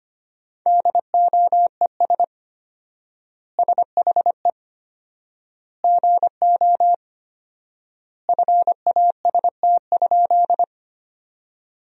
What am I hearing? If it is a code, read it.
Morse code, heard as a keyed tone, about 25 wpm, 700 Hz. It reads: DOES SHE GO FAST?
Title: 4x Word-Spacing - Sentences from Top 400 Words - 25wpm